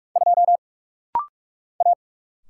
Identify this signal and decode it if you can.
Morse code, heard as a keyed tone, about 44 wpm, 700 Hz. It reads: 2 A
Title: Single Letter-Number Alt - Rapid-Fire 45wpm